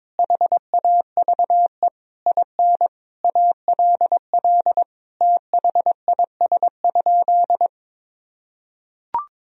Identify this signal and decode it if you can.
Morse code, heard as a keyed tone, about 21 wpm, 700 Hz. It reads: HAVE IN ALL THIS?
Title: Sentences from Top 1000 Words - Review Part 2 - 22wpm